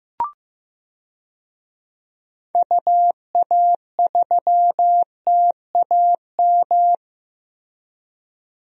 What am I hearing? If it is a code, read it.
Morse code, heard as a keyed tone, about 15 wpm, 700 Hz. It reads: UA3TAM
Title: DXCC-100 Call Signs - 6 Characters - Repeated 2x - 15wpm